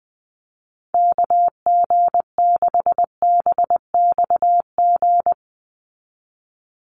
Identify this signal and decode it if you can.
Morse code, heard as a keyed tone, about 20 wpm, 700 Hz. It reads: KG6BXG
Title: USA Call Signs 20wpm